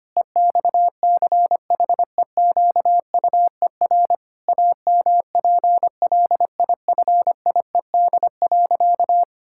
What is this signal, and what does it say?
Morse code, heard as a keyed tone, about 25 wpm, 700 Hz. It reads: EXCHEQUER AMPLIFIED.
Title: Pangram Challenge v2 - 25wpm